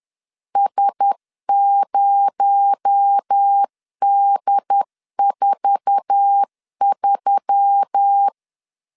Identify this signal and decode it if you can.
Morse code, heard as a keyed tone, about 10 wpm, 800 Hz. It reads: S0D43
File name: LEZIONI / 9 28 10wpm 2842